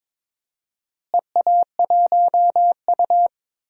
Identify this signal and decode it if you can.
Morse code, heard as a keyed tone, about 21 wpm, 700 Hz. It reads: EA1U
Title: DXCC-100 Call Signs - 4 Characters - Repeated 2x - 22wpm